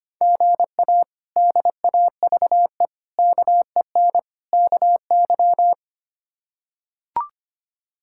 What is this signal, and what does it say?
Morse code, heard as a keyed tone, about 25 wpm, 700 Hz. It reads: GA DAVE KEN KY
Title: K1USN SST QSO Practice - 25wpm